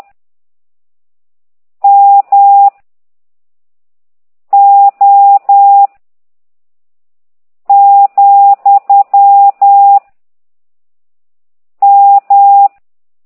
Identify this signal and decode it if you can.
Morse code, heard as a keyed tone, about 10 wpm, 800 Hz. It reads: M O , M